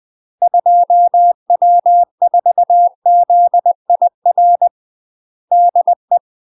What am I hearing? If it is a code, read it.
Morse code, heard as a keyed tone, about 20 wpm, 700 Hz. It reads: 2W4ZIR DE
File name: UK MASTER QSOs TRAINING / 20WPM / UK MASTER QSOs 0017